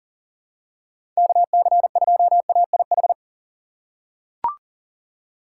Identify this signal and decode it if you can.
Morse code, heard as a keyed tone, about 40 wpm, 700 Hz. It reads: KC2AIH
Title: USA Call Signs - 4x Character Spacing - 40wpm